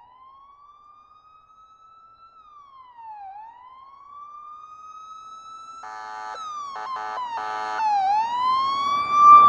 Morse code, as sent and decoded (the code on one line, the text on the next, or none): none